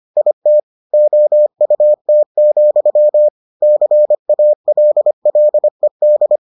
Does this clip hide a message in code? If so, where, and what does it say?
Yes: Morse code from 0.2 to 6.4 s, heard as a keyed tone, about 25 wpm, 600 Hz. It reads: IT OUT, CALLED